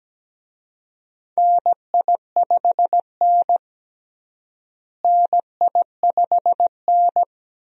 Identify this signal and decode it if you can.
Morse code, heard as a keyed tone, about 17 wpm, 700 Hz. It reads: NI5N NI5N
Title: USA Call Signs - 4 Characters - Repeated 2x - 17wpm